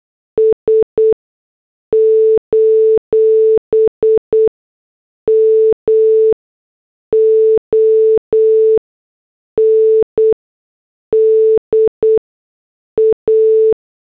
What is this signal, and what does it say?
Morse code, heard as a keyed tone, about 8 wpm, 440 Hz. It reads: S : M O N D A
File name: morse / headlines